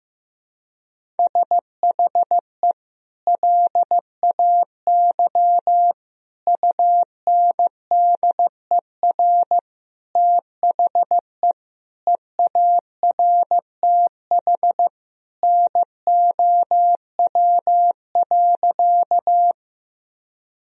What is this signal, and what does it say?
Morse code, heard as a keyed tone, about 15 wpm, 700 Hz. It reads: SHE LAY UNDER THE EARTH NOW.
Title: Sentences from Top 1000 Words - Review Part 2 - 15wpm